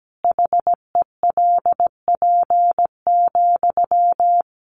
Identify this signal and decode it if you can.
Morse code, heard as a keyed tone, about 17 wpm, 700 Hz. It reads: HELP,
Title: Sentences from Top 1200 Words 17wpm